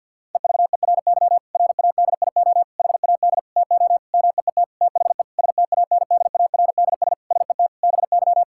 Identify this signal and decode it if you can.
Morse code, heard as a keyed tone, about 50 wpm, 700 Hz. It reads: EVERY RADIO HAD TO MEET THE STANDARDS SET BY
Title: Sentences from Top 1000 Words - Review Part 6 - 50wpm